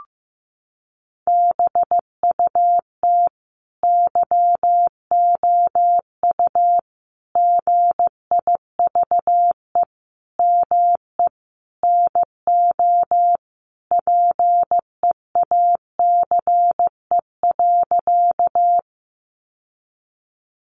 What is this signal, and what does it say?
Morse code, heard as a keyed tone, about 15 wpm, 700 Hz. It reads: BUT YOU GIVE ME NO PEACE.
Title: Sentences from Top 1500 Words 15wpm